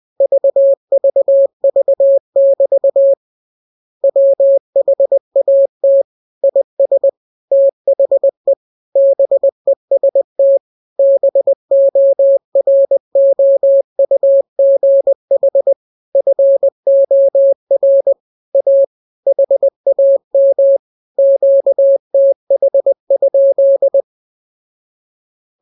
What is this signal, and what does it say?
Morse code, heard as a keyed tone, about 20 wpm, 550 Hz. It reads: VVV<BT> WHAT IS THE BEST BOROUGH FOR A HAM QTH?